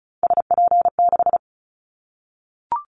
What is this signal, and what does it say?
Morse code, heard as a keyed tone, about 36 wpm, 700 Hz. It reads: SP6
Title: Callsign Prefixes for Top 25 DXCC - 35wpm